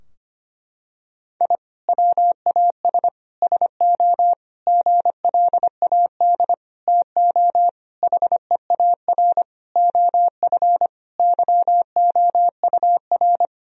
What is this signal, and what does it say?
Morse code, heard as a keyed tone, about 25 wpm, 700 Hz. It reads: I WAS SO GLAD TO HEAR OF YOUR
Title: Sentences from Top 1000 Words - Review Part 4 - 25wpm